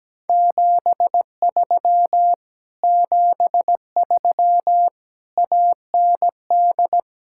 Morse code, heard as a keyed tone, about 17 wpm, 700 Hz. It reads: 73 73 AND
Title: Speed-Racing Common QSO Phrases Encore - 17wpm